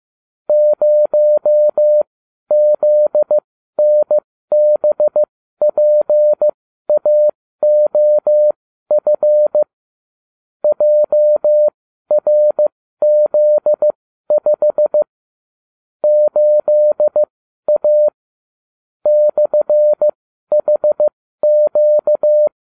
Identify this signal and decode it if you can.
Morse code, heard as a keyed tone, about 15 wpm, spaced as for 12 wpm, 600 Hz. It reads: 0ZNBPAOF JRZ5 8A /HQ